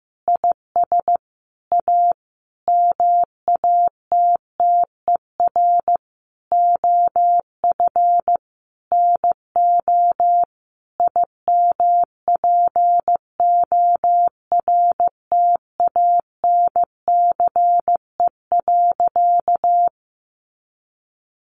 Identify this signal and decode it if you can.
Morse code, heard as a keyed tone, about 15 wpm, 700 Hz. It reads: IS A MATTER OF NO IMPORTANCE.
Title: Sentences from Top 1200 Words 15wpm